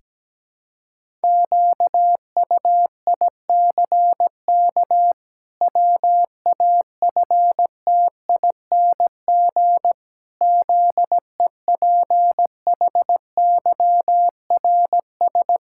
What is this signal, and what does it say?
Morse code, heard as a keyed tone, about 17 wpm, 700 Hz. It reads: QUICK WAFTING ZEPHYRS
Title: Pangram Challenge v2 - 17wpm